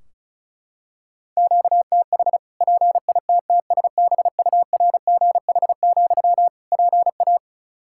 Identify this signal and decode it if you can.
Morse code, heard as a keyed tone, about 34 wpm, 700 Hz. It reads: QTH PITTSBURGH, PA
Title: Practice QSO - Exchange 1 - Beginning - 35wpm